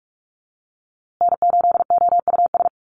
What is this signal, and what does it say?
Morse code, heard as a keyed tone, about 44 wpm, 700 Hz. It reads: N8OUS
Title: USA Call Signs Repeated 3x 45wpm